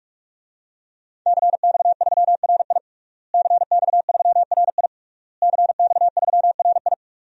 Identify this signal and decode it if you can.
Morse code, heard as a keyed tone, about 46 wpm, 700 Hz. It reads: CX3RI CX3RI CX3RI
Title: DXCC-100 Call Signs - 5 Characters - Repeated 3x - 45wpm